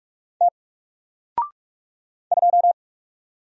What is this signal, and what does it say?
Morse code, heard as a keyed tone, about 45 wpm, 700 Hz. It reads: T 2
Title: Single Letter-Number 45wpm